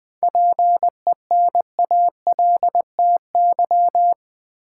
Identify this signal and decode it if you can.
Morse code, heard as a keyed tone, about 20 wpm, 700 Hz. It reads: PENALTY
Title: 7 Letter Words - First 3 Letters Given - 20wpm